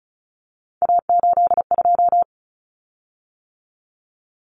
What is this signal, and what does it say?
Morse code, heard as a keyed tone, about 34 wpm, 700 Hz. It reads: A82
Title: Common Cut Numbers - 3-Digit 35wpm